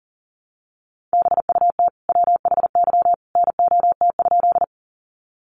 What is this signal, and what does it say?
Morse code, heard as a keyed tone, about 40 wpm, 700 Hz. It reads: BUT WHY NOT?
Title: Sentences from Top 600 Words 40wpm